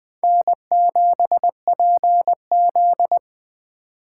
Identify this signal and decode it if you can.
Morse code, heard as a keyed tone, about 24 wpm, 700 Hz. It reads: N7PZ
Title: Speed Racing - USA 1x2 Call Signs 20wpm